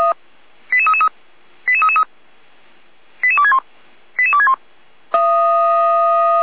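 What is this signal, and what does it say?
Five-tone selective call (ZVEI), heard as quick runs of five tones, tones of about 70 ms. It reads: 88333 88333 88261 88261
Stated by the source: ZVEI five-tone call received by SDR, from audio.f85360.feuerwehr-orlamuende.org, a fire brigade (Feuerwehr Orlamünde) server.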